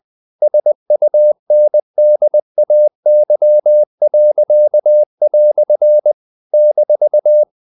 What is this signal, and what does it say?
Morse code, heard as a keyed tone, about 20 wpm, 600 Hz. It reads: SUNDAY." -
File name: CW / Practice / EnglishQuotations0320